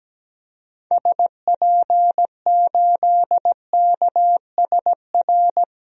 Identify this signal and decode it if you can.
Morse code, heard as a keyed tone, about 17 wpm, 700 Hz. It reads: SP8KSR
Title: DXCC-100 Call Signs - 6 Characters - Repeated 3x - 17wpm